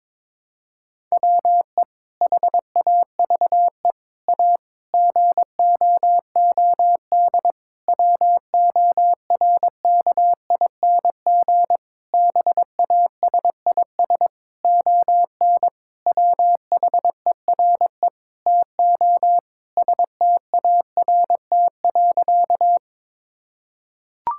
Morse code, heard as a keyed tone, about 21 wpm, 700 Hz. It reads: WE HAVE A GOOD WORKING BASIS ON WHERE TO START.
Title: Sentences from Top 1000 Words - Review Part 5 - 22wpm